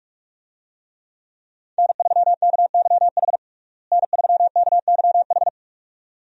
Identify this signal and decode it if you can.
Morse code, heard as a keyed tone, about 42 wpm, 700 Hz. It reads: N3KYH N3KYH
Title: USA Call Signs - 5 Characters - Repeated 2x - 45wpm